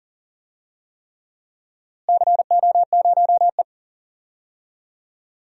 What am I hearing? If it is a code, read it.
Morse code, heard as a keyed tone, about 40 wpm, 700 Hz. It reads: CO0E
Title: DXCC-100 Call Signs - 4 Characters - Repeated 3x - 40wpm